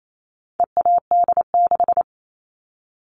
Morse code, heard as a keyed tone, about 28 wpm, 700 Hz. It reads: EAD6
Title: All Cut Numbers - 4-Digit 28wpm